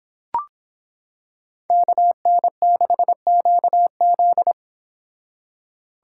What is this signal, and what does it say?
Morse code, heard as a keyed tone, about 26 wpm, 700 Hz. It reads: KN6QZ